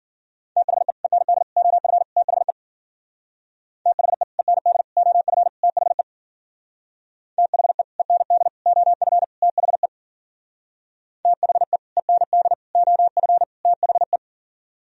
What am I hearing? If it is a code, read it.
Morse code, heard as a keyed tone, about 50 wpm, 700 Hz. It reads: THE END OF THE THE END OF THE THE END OF THE THE END OF THE